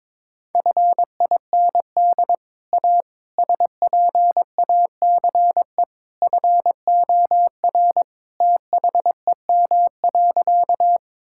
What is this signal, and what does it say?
Morse code, heard as a keyed tone, about 22 wpm, 700 Hz. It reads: FIND A SPACE FOR THEM.
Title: Sentences from Top 1000 Words - Review Part 5 - 22wpm